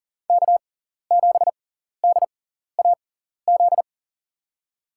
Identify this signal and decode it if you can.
Morse code, heard as a keyed tone, about 40 wpm, spaced as for 14 wpm, 700 Hz. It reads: K7DAZ
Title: USA Call Signs - 5 Characters - 6x Character Spacing - 40wpm